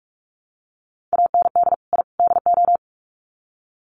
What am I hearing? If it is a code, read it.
Morse code, heard as a keyed tone, about 44 wpm, 700 Hz. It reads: AND I DO
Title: Sets of 3 Words 45wpm